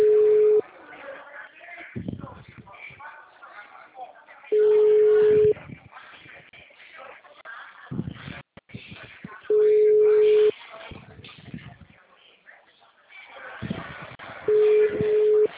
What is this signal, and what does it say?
Ringback tone (Europe, 425 Hz), with speech.